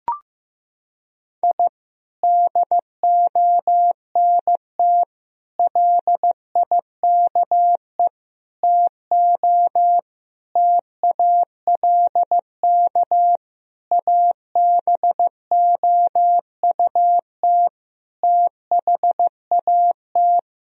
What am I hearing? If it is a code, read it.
Morse code, heard as a keyed tone, about 16 wpm, 700 Hz. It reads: I DONT LIKE TO TALK ABOUT THAT
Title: Sentences from Top 1000 Words - Review Part 1 - 15wpm